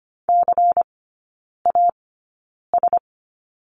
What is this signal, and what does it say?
Morse code, heard as a keyed tone, about 25 wpm, spaced as for 9 wpm, 700 Hz. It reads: CAS
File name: course / 25:10wpm / 093 Lesson 16e - Letter c - Three Characters 25wpm